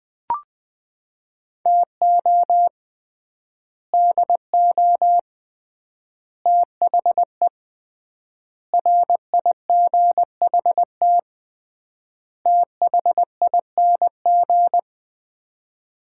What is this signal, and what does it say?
Morse code, heard as a keyed tone, about 20 wpm, 700 Hz. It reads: TO DO THE RIGHT THING
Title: 3x Word-Spacing - Sets of 5 Words - 20wpm